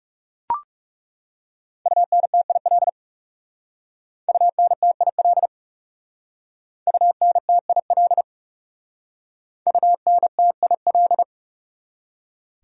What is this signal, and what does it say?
Morse code, heard as a keyed tone, about 36 wpm, 700 Hz. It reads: UNTIL UNTIL UNTIL UNTIL
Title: Speed-Racing Top 500 Words 30wpm